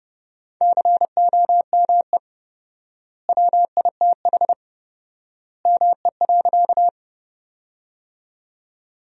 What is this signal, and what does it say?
Morse code, heard as a keyed tone, about 32 wpm, 700 Hz. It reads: COME WITH ME.
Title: 4x Word-Spacing - Sentences from Top 200 Words - 30wpm